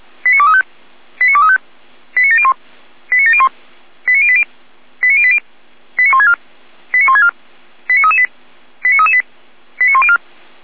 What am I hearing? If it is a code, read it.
Five-tone selective call (ZVEI), heard as quick runs of five tones, tones of about 80 ms. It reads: 89235 89235 89701 89701 89080 89080 89164 89164 89338 89338 89104
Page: ZVEI